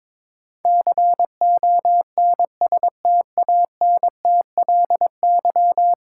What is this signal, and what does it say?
Morse code, heard as a keyed tone, about 22 wpm, 700 Hz. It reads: CONSTANTLY
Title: Sentences from Top 1000 Words - Review Part 7 - New Word Drill - 22wpm